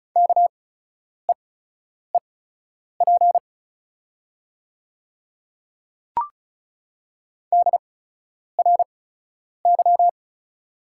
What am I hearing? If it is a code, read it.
Morse code, heard as a keyed tone, about 36 wpm, spaced as for 10 wpm, 700 Hz. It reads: KEEP DRY
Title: 8x Character Spacing - Top 500 Words - 35wpm